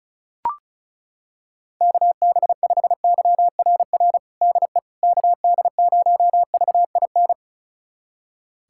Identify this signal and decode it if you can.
Morse code, heard as a keyed tone, about 34 wpm, 700 Hz. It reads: KB5YRR DE KD0VIN
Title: USA Callsign DE Callsign 35wpm